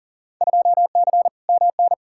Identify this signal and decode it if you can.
Morse code, heard as a keyed tone, about 38 wpm, 700 Hz. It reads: 2C MN
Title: Field Day Exchanges Encore - 40wpm